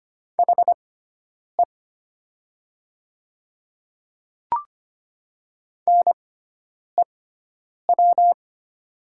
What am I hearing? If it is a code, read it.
Morse code, heard as a keyed tone, about 25 wpm, spaced as for 9 wpm, 700 Hz. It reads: HE NEW
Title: 6x Character Spacing - Top 300 Words - 25wpm